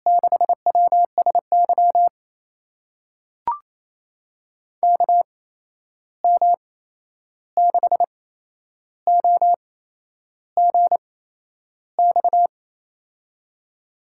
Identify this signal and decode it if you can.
Morse code, heard as a keyed tone, about 28 wpm, spaced as for 11 wpm, 700 Hz. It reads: 6WSY KM6OGX
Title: USA Call Signs - 6 Characters - 8x Character Spacing - 28wpm